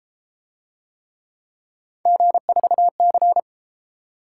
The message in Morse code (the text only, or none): G4C